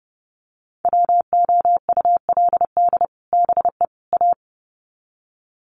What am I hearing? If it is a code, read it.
Morse code, heard as a keyed tone, about 28 wpm, 700 Hz. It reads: WOULD BE A
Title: Sets of 3 Words 30wpm